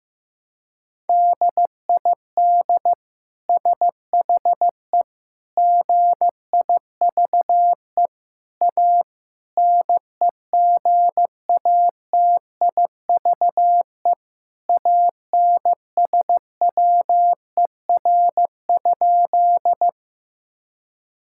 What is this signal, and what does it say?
Morse code, heard as a keyed tone, about 15 wpm, 700 Hz. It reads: DID SHE GIVE A NEGATIVE ANSWER?
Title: Sentences from Top 1000 Words - Review Part 6 - 15wpm